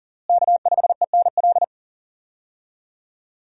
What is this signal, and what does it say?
Morse code, heard as a keyed tone, about 40 wpm, 690 Hz. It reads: K5ENL
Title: USA Call Signs - 5 Characters - Repeated 3x - 40wpm